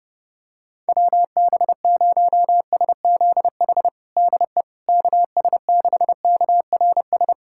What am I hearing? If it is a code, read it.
Morse code, heard as a keyed tone, about 32 wpm, 700 Hz. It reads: WB0SZH DE KS6KRS
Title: USA Callsign DE Callsign 30wpm